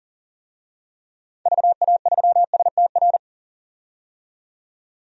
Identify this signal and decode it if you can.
Morse code, heard as a keyed tone, about 40 wpm, 700 Hz. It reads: VA3STR
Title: DXCC-100 Call Signs - 6 Characters - 40wpm